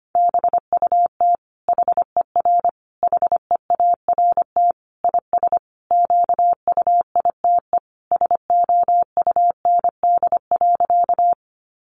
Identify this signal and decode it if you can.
Morse code, heard as a keyed tone, about 25 wpm, 700 Hz. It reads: BUT HER HEART IS QUITE SOUND.
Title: Sentences from Top 1200 Words 25wpm